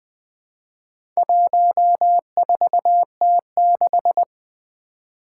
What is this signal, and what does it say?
Morse code, heard as a keyed tone, about 20 wpm, 700 Hz. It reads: 14T6
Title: Four Letter-Number Combinations - 20wpm